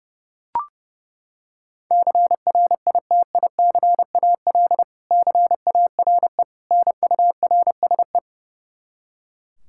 Morse code, heard as a keyed tone, about 29 wpm, 700 Hz. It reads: CRITICAL CARE NURSE
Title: Practice QSO - Exchange 3 - More Info - 30wpm